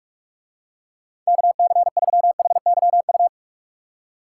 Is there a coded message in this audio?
Yes: Morse code from 1.3 to 3.3 s, heard as a keyed tone, about 49 wpm, 700 Hz. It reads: KC3HYU